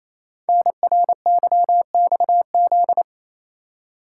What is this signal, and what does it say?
Morse code, heard as a keyed tone, about 27 wpm, 700 Hz. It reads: NRYXZ
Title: Code Groups - 5 Letters - 28wpm